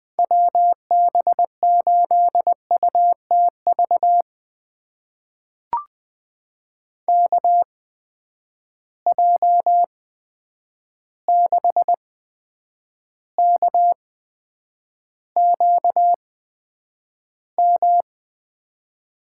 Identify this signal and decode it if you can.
Morse code, heard as a keyed tone, about 20 wpm, spaced as for 9 wpm, 700 Hz. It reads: WB8UTV KJ6KQM
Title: USA Call Signs - 6 Characters - 8x Character Spacing - 20wpm